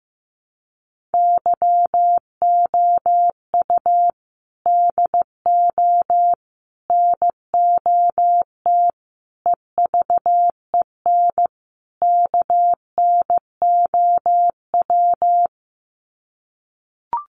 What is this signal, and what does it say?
Morse code, heard as a keyed tone, about 15 wpm, 700 Hz. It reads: YOU DO NOT EVEN KNOW
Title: Sets of 5 Words Encore2 15wpm